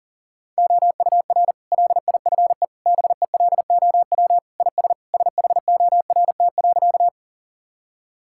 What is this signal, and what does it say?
Morse code, heard as a keyed tone, about 40 wpm, 700 Hz. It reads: OUR LIFE BELOW IS SHORT.
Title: Sentences from Top 1000 Words - Review Part 5 - 40wpm